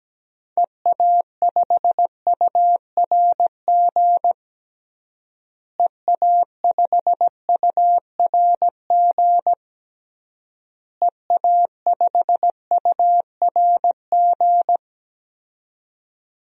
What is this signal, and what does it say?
Morse code, heard as a keyed tone, about 17 wpm, 700 Hz. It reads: EA5URG EA5URG EA5URG
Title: DXCC-100 Call Signs - 6 Characters - Repeated 3x - 17wpm